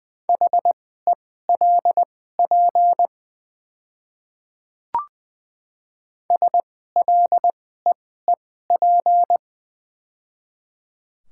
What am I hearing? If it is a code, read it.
Morse code, heard as a keyed tone, about 20 wpm, spaced as for 14 wpm, 700 Hz. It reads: HELP SLEEP
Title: 2x Character Spacing - Top 500 Words - 20wpm